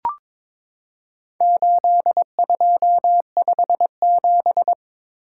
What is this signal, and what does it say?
Morse code, heard as a keyed tone, about 22 wpm, 700 Hz. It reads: 8257
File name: practice / 22wpm / Four Letter-Number Combinations - 22wpm